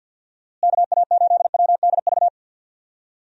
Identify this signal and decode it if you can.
Morse code, heard as a keyed tone, about 52 wpm, 700 Hz. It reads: KA8WDV
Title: USA Call Signs - 6 Characters - Repeated 3x - 50wpm